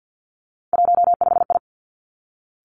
Morse code, heard as a keyed tone, about 50 wpm, 700 Hz. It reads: 15I
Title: Three Challenging Letter-Number Combinations 50wpm